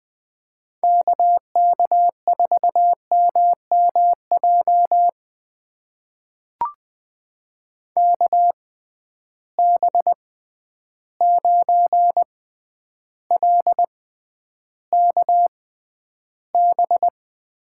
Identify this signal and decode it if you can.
Morse code, heard as a keyed tone, about 20 wpm, spaced as for 10 wpm, 700 Hz. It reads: KK4MMJ KB9LKB
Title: USA Call Signs - 6x Character Spacing - 20wpm